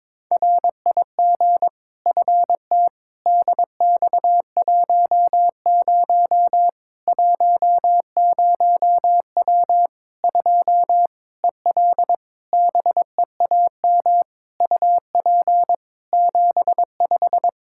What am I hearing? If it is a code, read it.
Morse code, heard as a keyed tone, about 22 wpm, 700 Hz. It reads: RIG FT DX10 10W 2 EL BEAM UP 75
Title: K1USN SST QSO Practice 22wpm